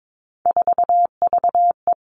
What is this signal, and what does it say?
Morse code, heard as a keyed tone, about 22 wpm, 700 Hz. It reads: 4VE
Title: Three Challenging Letter-Number Combinations 22wpm